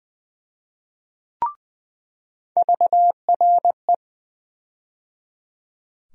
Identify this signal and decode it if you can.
Morse code, heard as a keyed tone, about 20 wpm, 700 Hz. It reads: VRE
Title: USA Call Sign Postfixes - 3 Letters 20wpm